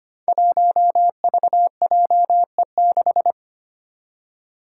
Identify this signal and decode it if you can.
Morse code, heard as a keyed tone, about 25 wpm, 700 Hz. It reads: 1VJE6
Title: Five Challenging Letter-Number Combinations 25wpm